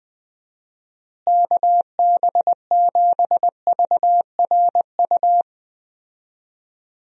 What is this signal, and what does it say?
Morse code, heard as a keyed tone, about 21 wpm, 700 Hz. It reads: KB7VRU
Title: USA Call Signs - 6 Characters - Repeated 2x - 20wpm